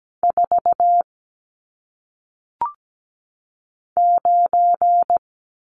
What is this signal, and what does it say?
Morse code, heard as a keyed tone, about 17 wpm, 700 Hz. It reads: 4 9
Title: Numbers - 1-Digit 17wpm